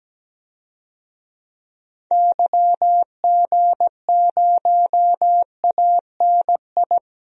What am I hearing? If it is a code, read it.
Morse code, heard as a keyed tone, about 17 wpm, 700 Hz. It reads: YG0ANI
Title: DXCC-100 Call Signs - 6 Characters - Repeated 3x - 17wpm